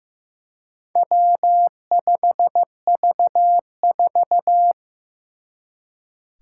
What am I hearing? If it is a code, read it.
Morse code, heard as a keyed tone, about 15 wpm, 700 Hz. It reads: W5V4
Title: Four Challenging Letter-Number Combinations 15wpm